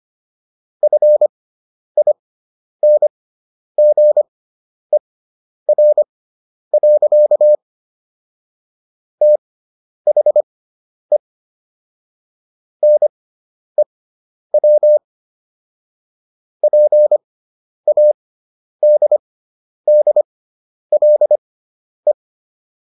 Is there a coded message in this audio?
Yes: Morse code from 0.8 to 22.1 s, heard as a keyed tone, about 25 wpm, spaced as for 10 wpm, 600 Hz. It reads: FINGER. THE NEW PADDLE